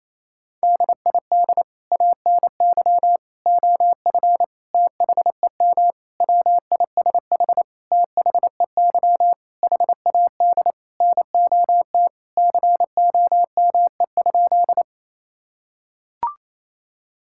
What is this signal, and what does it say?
Morse code, heard as a keyed tone, about 28 wpm, 700 Hz. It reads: DID ANY OF THEM WISH THEY HAD NOT COME?
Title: Sentences from Top 1600 Words 28wpm